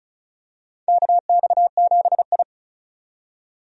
Morse code, heard as a keyed tone, about 36 wpm, 700 Hz. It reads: KX7I
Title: USA Call Signs - 4 Characters - 35wpm